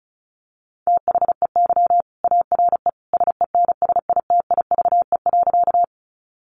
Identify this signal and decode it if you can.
Morse code, heard as a keyed tone, about 34 wpm, 700 Hz. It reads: THEY ARE SENSITIVE.
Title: Sentences from Top 1400 Words 35wpm